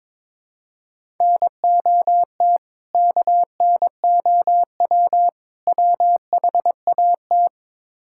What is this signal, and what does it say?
Morse code, heard as a keyed tone, about 22 wpm, 700 Hz. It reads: NOT KNOW WHAT
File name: practice / 22wpm / Build A Phrase - 5-Word Phrases - 22wpm